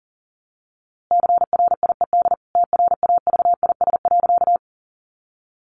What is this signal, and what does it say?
Morse code, heard as a keyed tone, about 40 wpm, 700 Hz. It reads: CRIED TRAVIS.
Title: A Sound Of Thunder 40wpm